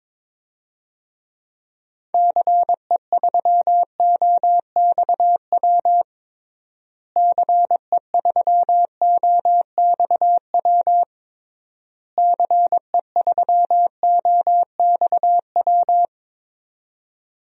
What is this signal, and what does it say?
Morse code, heard as a keyed tone, about 22 wpm, 700 Hz. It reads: CE3OXW CE3OXW CE3OXW
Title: DXCC-100 Call Signs - 6 Characters - Repeated 3x - 22wpm